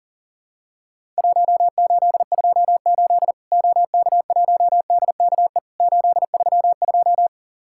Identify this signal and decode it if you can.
Morse code, heard as a keyed tone, about 40 wpm, 700 Hz. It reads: 1828 OK1DKE 832
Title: Worked All Europe DX Contest - QTC Exchange - 40wpm